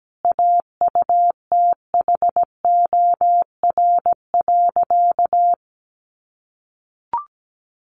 Morse code, heard as a keyed tone, about 17 wpm, 700 Hz. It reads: AUTHOR.
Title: Sentences from Top 1200 Words 17wpm